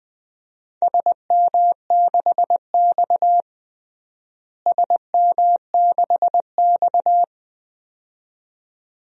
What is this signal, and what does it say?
Morse code, heard as a keyed tone, about 20 wpm, 700 Hz. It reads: SM6X SM6X
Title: DXCC-100 Call Signs - 4 Characters - Repeated 2x - 20wpm